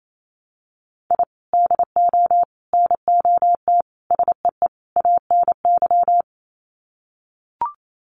Morse code, heard as a keyed tone, about 28 wpm, 700 Hz. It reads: I DO NOT SEE ANY
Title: Sets of 5 Words 28wpm